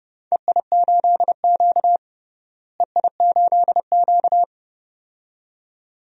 Morse code, heard as a keyed tone, about 30 wpm, 700 Hz. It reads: EI8Q EI8Q
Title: DXCC-100 Call Signs - 4 Characters - Repeated 2x - 30wpm